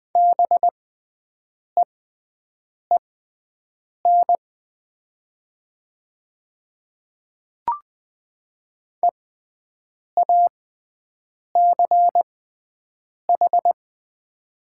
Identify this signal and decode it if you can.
Morse code, heard as a keyed tone, about 20 wpm, spaced as for 7 wpm, 690 Hz. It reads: BEEN EACH
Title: 6x Character Spacing - Top 100 Words - 20wpm